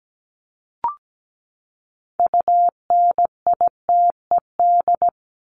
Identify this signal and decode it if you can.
Morse code, heard as a keyed tone, about 17 wpm, 700 Hz. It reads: UNITED
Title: Top 501-750 Words 17wpm